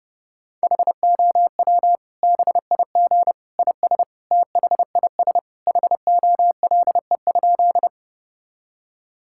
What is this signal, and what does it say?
Morse code, heard as a keyed tone, about 31 wpm, 700 Hz. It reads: HOW BIG IS THIS HOLE?
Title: Sentences from Top 1600 Words 30wpm